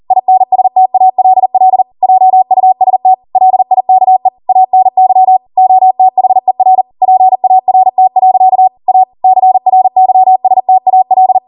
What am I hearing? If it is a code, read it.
Morse code, heard as a keyed tone, about 40 wpm, 750 Hz. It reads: INSTALL JUST LIKE ANY OTHER PART. A CRYSTAL
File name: Morse / 40WPM